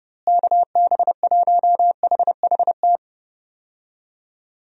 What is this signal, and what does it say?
Morse code, heard as a keyed tone, about 32 wpm, 700 Hz. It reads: KB1HHT
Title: USA Call Signs - 6 Characters - Repeated 3x - 30wpm